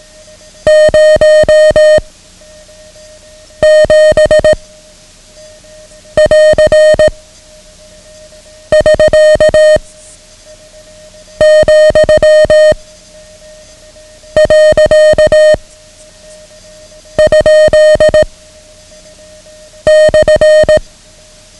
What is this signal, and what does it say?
Morse code, heard as a keyed tone, about 18 wpm, 620 Hz. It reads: 0 7 <AR> <SK> , . ? /
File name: ham radio / CodeQuick / Code Quick 09